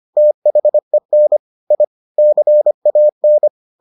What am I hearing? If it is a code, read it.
Morse code, heard as a keyed tone, about 25 wpm, 600 Hz. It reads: THEN I CAN